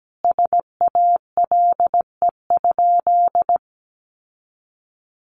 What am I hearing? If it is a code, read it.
Morse code, heard as a keyed tone, about 17 wpm, 700 Hz. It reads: SALE?
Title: Sentences from Top 1400 Words 17wpm